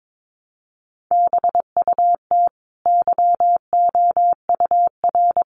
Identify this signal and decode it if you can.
Morse code, heard as a keyed tone, about 22 wpm, 700 Hz. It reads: BUT YOUR
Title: Sentences from Top 1200 Words 22wpm